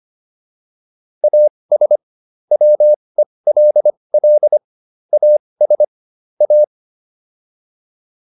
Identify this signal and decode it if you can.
Morse code, heard as a keyed tone, about 25 wpm, spaced as for 20 wpm, 600 Hz. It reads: AS WELL AS A